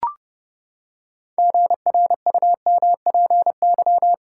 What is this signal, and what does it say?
Morse code, heard as a keyed tone, about 32 wpm, 700 Hz. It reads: GRUMPY
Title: Pangram Challenge v2 - 30wpm